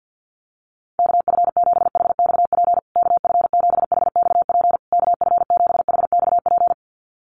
Morse code, heard as a keyed tone, about 50 wpm, 700 Hz. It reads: KF7HXP KF7HXP KF7HXP
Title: USA Call Signs Repeated 3x 50wpm